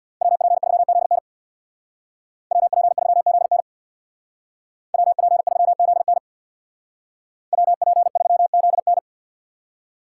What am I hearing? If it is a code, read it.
Morse code, heard as a keyed tone, about 62 wpm, 700 Hz. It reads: WP3ZN WP3ZN WP3ZN WP3ZN
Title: Speed Racing - USA Call Signs 50wpm